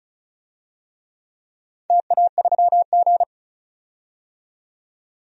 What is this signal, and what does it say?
Morse code, heard as a keyed tone, about 36 wpm, 700 Hz. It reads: TA3G
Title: DXCC-100 Call Signs - 4 Characters - 35wpm